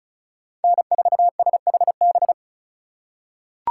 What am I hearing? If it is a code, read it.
Morse code, heard as a keyed tone, about 34 wpm, 700 Hz. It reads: N4SHB
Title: USA Call Signs - 3x Character Spacing - 35wpm